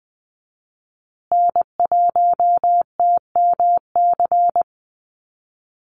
Morse code, heard as a keyed tone, about 20 wpm, 700 Hz. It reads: N1TMC
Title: USA Call Signs Repeated 2x 20wpm